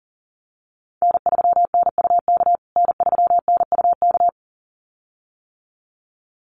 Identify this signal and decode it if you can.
Morse code, heard as a keyed tone, about 40 wpm, 700 Hz. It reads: N3NUK N3NUK
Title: USA Call Signs Repeated 2x 40wpm